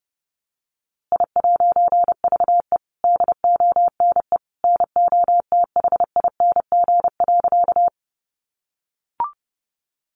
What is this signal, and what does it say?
Morse code, heard as a keyed tone, about 30 wpm, 700 Hz. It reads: I'VE DONE NOTHING.
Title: A Sound Of Thunder 30wpm